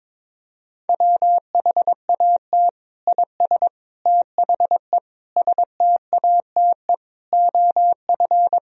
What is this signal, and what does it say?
Morse code, heard as a keyed tone, about 22 wpm, 700 Hz. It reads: WHAT IS THE STATE OF
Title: Sentences from Top 1000 Words - Review Part 4 - 22wpm